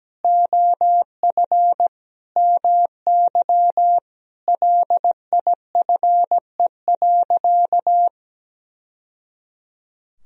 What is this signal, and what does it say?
Morse code, heard as a keyed tone, about 17 wpm, 700 Hz. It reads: OF MY LIFE.
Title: Sentences from Top 1000 Words - Review Part 2 - 17wpm